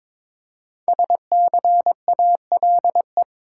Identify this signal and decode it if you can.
Morse code, heard as a keyed tone, about 22 wpm, 700 Hz. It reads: SCALE
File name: 5 Letter Words - First 3 Letters Given - 22wpm